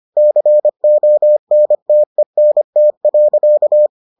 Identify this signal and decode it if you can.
Morse code, heard as a keyed tone, about 25 wpm, 600 Hz. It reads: CONTENT.